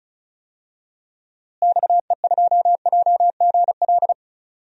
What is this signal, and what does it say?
Morse code, heard as a keyed tone, about 36 wpm, 700 Hz. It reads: XE2JGL
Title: DXCC-100 Call Signs - 6 Characters - Repeated 3x - 35wpm